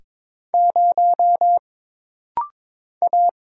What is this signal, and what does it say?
Morse code, heard as a keyed tone, about 22 wpm, 700 Hz. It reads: 0 A
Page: This practice set is composed of Letters, Numbers, common punctuations, and common prosigns.